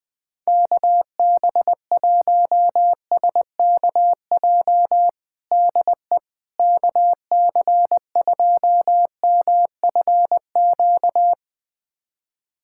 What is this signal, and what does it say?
Morse code, heard as a keyed tone, about 20 wpm, 700 Hz. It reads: KB1SKJ DE KC2MFQ
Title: USA Callsign DE Callsign 20wpm